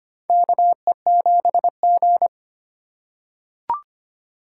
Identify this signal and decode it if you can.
Morse code, heard as a keyed tone, about 25 wpm, 700 Hz. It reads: KE7G
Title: USA Call Signs - 3x Character Spacing - 25wpm